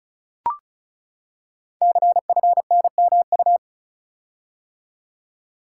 Morse code, heard as a keyed tone, about 36 wpm, 690 Hz. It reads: CFNMU
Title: Code Groups - 5 Letters - 35wpm